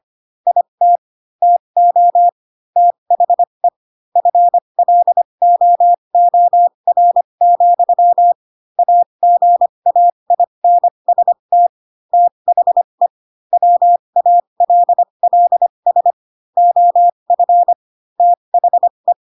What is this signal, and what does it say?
Morse code, heard as a keyed tone, about 25 wpm, spaced as for 22 wpm, 700 Hz. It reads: IT TO THE FLOOR, AGAINST THE WALLS OF THE